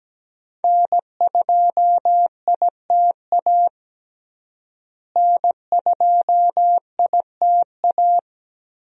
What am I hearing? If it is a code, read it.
Morse code, heard as a keyed tone, about 17 wpm, 700 Hz. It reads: N2ITA N2ITA
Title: USA Call Signs - 5 Characters - Repeated 3x - 17wpm